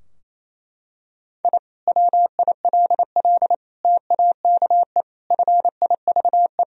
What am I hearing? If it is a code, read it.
Morse code, heard as a keyed tone, about 27 wpm, 700 Hz. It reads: I WILL TAKE FIVE
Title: Sentences from Top 1000 Words - Review Part 2 - 28wpm